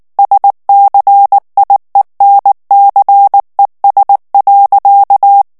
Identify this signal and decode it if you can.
Morse code, heard as a keyed tone, about 19 wpm, 800 Hz. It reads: SCIENCES.
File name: morse mp3 / FRANCAIS TEXTE CLAIR 1 VITESSE 18 MOTS MINUTES